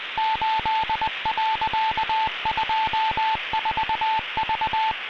Morse code, heard as a keyed tone, about 20 wpm, 880 Hz, with radio noise. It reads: 8.24V